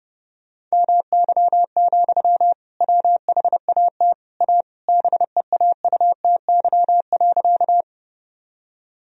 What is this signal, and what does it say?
Morse code, heard as a keyed tone, about 30 wpm, 700 Hz. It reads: MY, WHAT A BEAUTY.
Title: Sentences from Top 1000 Words - Review Part 2 - 30wpm